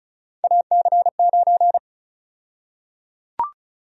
Morse code, heard as a keyed tone, about 36 wpm, 700 Hz. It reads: AC9
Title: USA Call Sign Prefixes - 35wpm